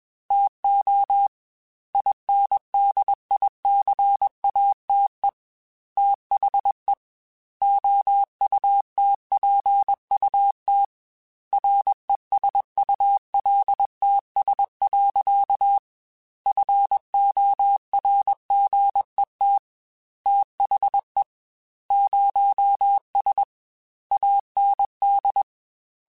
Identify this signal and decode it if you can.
Morse code, heard as a keyed tone, about 21 wpm, 800 Hz. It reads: TO INDICATE THE OUTPUT RESULTS. FORGET THE 0S AND
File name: Technical Subjects / Technical Subjects-mp3s / Gates Intro @20